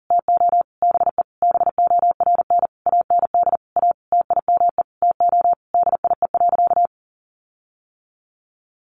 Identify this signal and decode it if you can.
Morse code, heard as a keyed tone, about 37 wpm, 700 Hz. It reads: TO BE BORN AND A TIME TO DIE.
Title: Sentences from Top 1300 Words 40wpm